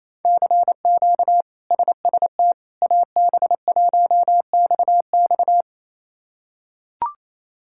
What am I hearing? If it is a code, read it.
Morse code, heard as a keyed tone, about 28 wpm, 700 Hz. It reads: CQ SST AB1XX
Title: K1USN SST QSO Practice - 28wpm